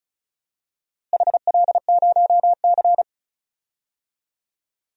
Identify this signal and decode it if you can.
Morse code, heard as a keyed tone, about 34 wpm, 700 Hz. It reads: HL0C
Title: DXCC-100 Call Signs - 4 Characters - Repeated 3x - 35wpm